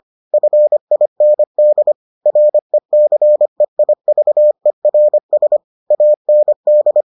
Morse code, heard as a keyed tone, about 25 wpm, 600 Hz. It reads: FIND RECEIVERS AND